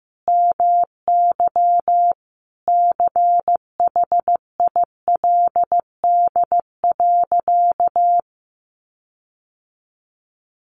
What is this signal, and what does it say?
Morse code, heard as a keyed tone, about 15 wpm, 700 Hz. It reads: MY CHILD.
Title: Sentences from Top 1300 Words 15wpm